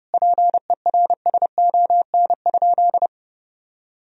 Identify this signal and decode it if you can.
Morse code, heard as a keyed tone, about 29 wpm, 700 Hz. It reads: PERSON?
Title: Sentences from Top 1000 Words - Review Part 4 - 30wpm